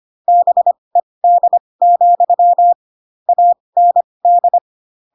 Morse code, heard as a keyed tone, about 25 wpm, spaced as for 20 wpm, 700 Hz. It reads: BED, AND